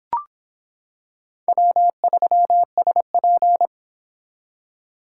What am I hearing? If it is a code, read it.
Morse code, heard as a keyed tone, about 26 wpm, 700 Hz. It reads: W3SP